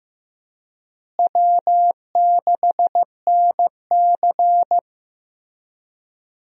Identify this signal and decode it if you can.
Morse code, heard as a keyed tone, about 15 wpm, 700 Hz. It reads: W6NC
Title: USA Call Signs - 4 Characters - Repeated 3x - 15wpm